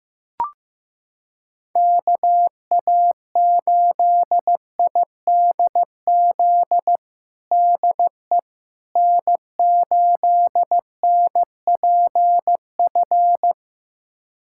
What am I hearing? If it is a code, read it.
Morse code, heard as a keyed tone, about 16 wpm, 700 Hz. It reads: KA8IDZ DE N8NPF
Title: USA Callsign DE Callsign 15wpm